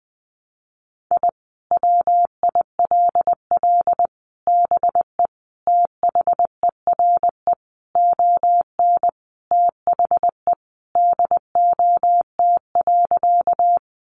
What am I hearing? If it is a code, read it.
Morse code, heard as a keyed tone, about 20 wpm, 700 Hz. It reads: I WILL BE THERE ON THE DOT.
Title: Sentences from Top 1500 Words 20wpm